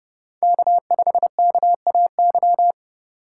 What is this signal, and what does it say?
Morse code, heard as a keyed tone, about 31 wpm, 700 Hz. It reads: K5KAY
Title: USA Call Signs - 5 Characters - 2x Character Spacing - 30wpm